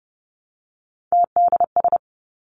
Morse code, heard as a keyed tone, about 30 wpm, 700 Hz. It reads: TDS
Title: Three Challenging Letter-Number Combinations 30wpm